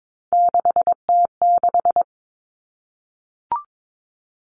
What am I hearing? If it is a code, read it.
Morse code, heard as a keyed tone, about 22 wpm, 700 Hz. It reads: 6T6
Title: All Cut Numbers - 3-Digit 22wpm